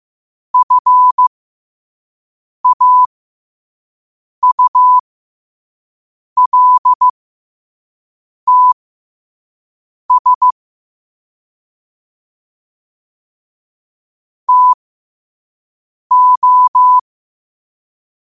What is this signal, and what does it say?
Morse code, heard as a keyed tone, about 15 wpm, spaced as for 5 wpm, 1000 Hz. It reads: FAULTS TO
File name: qotdfiles / 5wpm